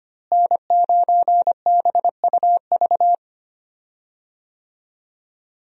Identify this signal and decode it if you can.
Morse code, heard as a keyed tone, about 25 wpm, 700 Hz. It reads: N9BUV
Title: USA Call Signs - 5 Characters - Repeated 3x - 25wpm